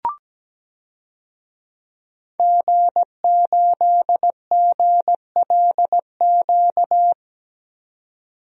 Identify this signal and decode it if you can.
Morse code, heard as a keyed tone, about 17 wpm, 700 Hz. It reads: G8GLQ
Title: DXCC-100 Call Signs - 5 Characters - 17wpm